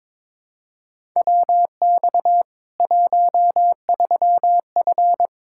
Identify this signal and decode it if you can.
Morse code, heard as a keyed tone, about 21 wpm, 700 Hz. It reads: WX 13F
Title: K1USN SST QSO Practice 22wpm